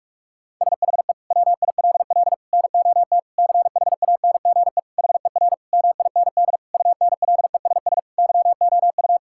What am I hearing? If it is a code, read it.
Morse code, heard as a keyed tone, about 46 wpm, 700 Hz. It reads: SHE WILL NOT CHANGE HER MIND UNLESS YOU
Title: Sentences from Top 1000 Words - Review Part 6 - 45wpm